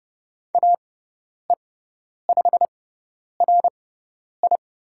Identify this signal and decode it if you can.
Morse code, heard as a keyed tone, about 30 wpm, spaced as for 10 wpm, 700 Hz. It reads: AE5RI
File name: Lesson 16g - Letter c - Call Signs 30wpm